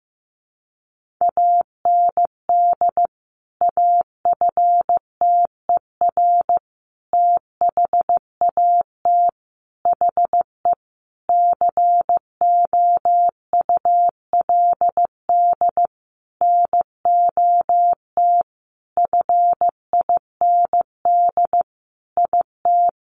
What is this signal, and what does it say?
Morse code, heard as a keyed tone, about 14 wpm, 700 Hz. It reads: AND AFTER THAT HE COULD NOT FIND IT
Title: Sentences from Top 1400 Words 15wpm